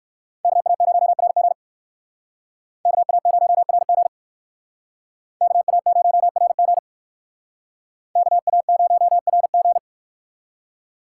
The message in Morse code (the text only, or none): KA0RG KA0RG KA0RG KA0RG